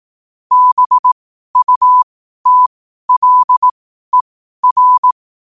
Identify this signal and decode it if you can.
Morse code, heard as a keyed tone, about 18 wpm, spaced as for 12 wpm, 1000 Hz. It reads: BUTLER
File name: qotdfiles / 12wpm